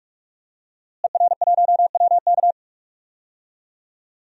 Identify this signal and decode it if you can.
Morse code, heard as a keyed tone, about 42 wpm, 700 Hz. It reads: ER1WK
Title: DXCC-100 Call Signs - 5 Characters - Repeated 3x - 45wpm